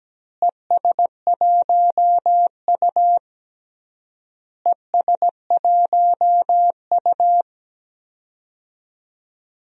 Morse code, heard as a keyed tone, about 17 wpm, 700 Hz. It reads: ES1U ES1U
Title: DXCC-100 Call Signs - 4 Characters - Repeated 3x - 17wpm